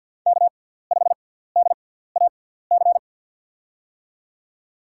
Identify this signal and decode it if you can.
Morse code, heard as a keyed tone, about 52 wpm, spaced as for 17 wpm, 700 Hz. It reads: K5DAC
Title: USA Call Signs - 6x Character Spacing - 50wpm